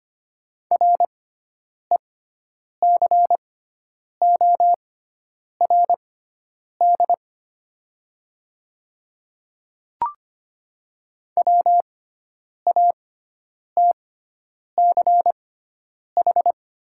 Morse code, heard as a keyed tone, about 25 wpm, spaced as for 9 wpm, 700 Hz. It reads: RECORD WATCH
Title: 6x Character Spacing - Top 500 Words - 25wpm